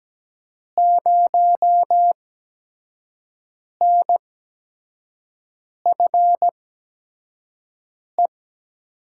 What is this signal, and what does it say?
Morse code, heard as a keyed tone, about 17 wpm, spaced as for 5 wpm, 700 Hz. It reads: 0NFE
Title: USA Call Signs - 6 Characters - 8x Character Spacing - 17wpm